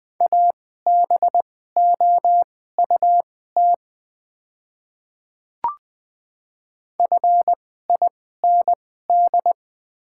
Morse code, heard as a keyed tone, about 20 wpm, spaced as for 14 wpm, 700 Hz. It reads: ABOUT FIND
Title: 2x Character Spacing - Top 100 Words - 20wpm